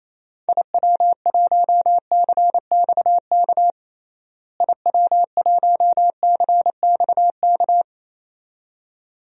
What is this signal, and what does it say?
Morse code, heard as a keyed tone, about 28 wpm, 700 Hz. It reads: IW1CXK IW1CXK
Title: DXCC-100 Call Signs - 6 Characters - Repeated 2x - 28wpm